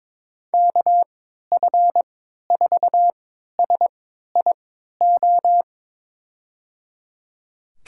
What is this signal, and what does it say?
Morse code, heard as a keyed tone, about 22 wpm, spaced as for 12 wpm, 700 Hz. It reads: KF4SIO
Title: USA Call Signs - 6 Characters - 3x Character Spacing - 22wpm